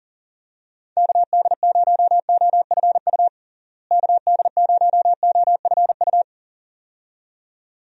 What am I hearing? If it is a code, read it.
Morse code, heard as a keyed tone, about 40 wpm, 700 Hz. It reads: KD0OFU KD0OFU
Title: USA Call Signs - 6 Characters - Repeated 2x - 40wpm